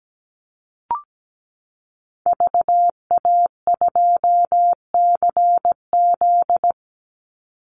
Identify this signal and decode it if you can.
Morse code, heard as a keyed tone, about 17 wpm, 700 Hz. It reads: VA2CZ
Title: Call Signs Repeated 3x - 17wpm